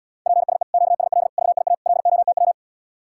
Morse code, heard as a keyed tone, about 79 wpm, 700 Hz. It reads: PSE QSL VIA LOTW